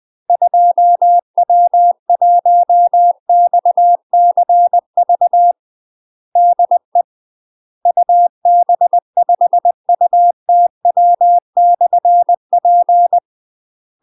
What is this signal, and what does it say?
Morse code, heard as a keyed tone, about 20 wpm, 700 Hz. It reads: 2W1XCV DE UB5UTW/P
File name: UK MASTER QSOs TRAINING / 20WPM / UK MASTER QSOs 0057